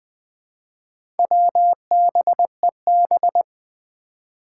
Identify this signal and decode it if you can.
Morse code, heard as a keyed tone, about 20 wpm, 700 Hz. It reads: WBEB
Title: Four Challenging Letter-Number Combinations 20wpm